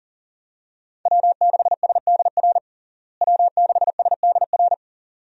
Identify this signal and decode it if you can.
Morse code, heard as a keyed tone, about 40 wpm, 700 Hz. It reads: W6SDR W6SDR
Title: USA Call Signs - 5 Characters - Repeated 2x - 40wpm